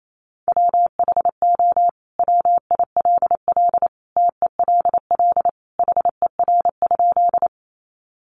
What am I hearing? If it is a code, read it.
Morse code, heard as a keyed tone, about 28 wpm, 700 Hz. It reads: WHO WILL TELL HER?
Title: Sentences from Top 200 Words 28wpm